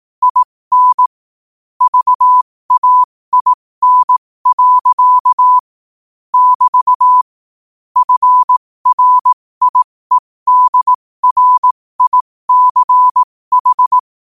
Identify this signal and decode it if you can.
Morse code, heard as a keyed tone, about 18 wpm, spaced as for 15 wpm, 1000 Hz. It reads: IN VAIN. <BT> FRIEDRICH
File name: qotdfiles / 15wpm